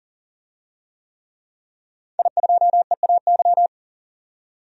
Morse code, heard as a keyed tone, about 40 wpm, 700 Hz. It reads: I2EAY
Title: DXCC-100 Call Signs - 5 Characters - 40wpm